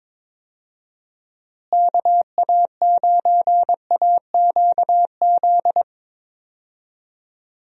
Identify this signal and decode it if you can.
Morse code, heard as a keyed tone, about 22 wpm, 700 Hz. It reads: KA9AQZ
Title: USA Call Signs - 6 Characters - Repeated 3x - 22wpm